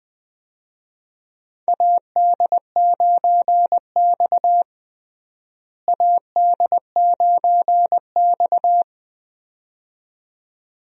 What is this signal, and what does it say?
Morse code, heard as a keyed tone, about 20 wpm, 700 Hz. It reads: AD9X AD9X
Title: USA Call Signs - 4 Characters - Repeated 2x - 20wpm